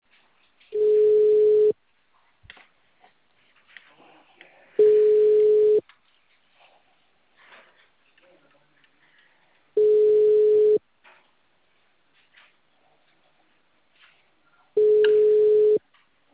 Ringback tone (Europe, 425 Hz).
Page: Porto Ferreira - SP